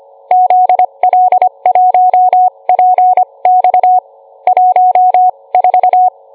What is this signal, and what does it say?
Morse code, heard as a keyed tone, about 25 wpm, spaced as for 22 wpm, 730 Hz. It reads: ZL1PX 14